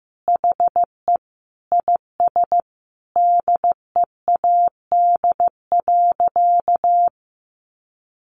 Morse code, heard as a keyed tone, about 15 wpm, 700 Hz. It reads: HE IS DEAD.
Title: Sentences from Top 1500 Words 15wpm